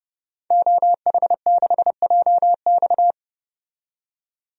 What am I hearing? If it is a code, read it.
Morse code, heard as a keyed tone, about 30 wpm, 700 Hz. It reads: OH6JX
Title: DXCC-100 Call Signs - 5 Characters - 30wpm